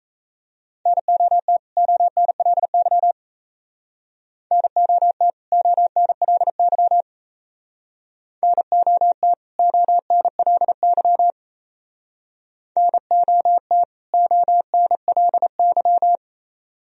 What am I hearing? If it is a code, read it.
Morse code, heard as a keyed tone, about 34 wpm, 700 Hz. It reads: NOT ONLY NOT ONLY NOT ONLY NOT ONLY
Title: Speed-Racing - Sets of 2 Words 28wpm